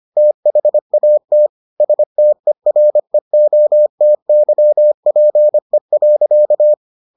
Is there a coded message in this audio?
Yes: Morse code from 0.2 to 6.7 s, heard as a keyed tone, about 25 wpm, 600 Hz. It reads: THAT STEREOTYPE.